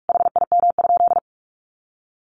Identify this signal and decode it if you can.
Morse code, heard as a keyed tone, about 44 wpm, 700 Hz. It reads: HIM?
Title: Sentences from Top 1500 Words 45wpm